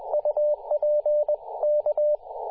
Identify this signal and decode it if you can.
Morse code, heard as a keyed tone, about 20 wpm, 600 Hz, with radio noise. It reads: UP K
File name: posts / 2017-sep-21-0819zt88xa-10-116mhz-cw